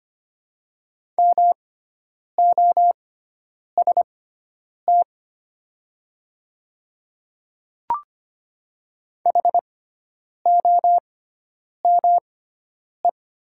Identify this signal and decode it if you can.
Morse code, heard as a keyed tone, about 25 wpm, spaced as for 9 wpm, 700 Hz. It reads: MOST HOME
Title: 6x Character Spacing - Top 200 Words - 25wpm